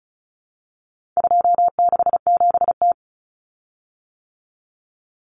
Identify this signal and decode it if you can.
Morse code, heard as a keyed tone, about 34 wpm, 700 Hz. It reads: 267T
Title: Common Cut Numbers - 4-Digit 35wpm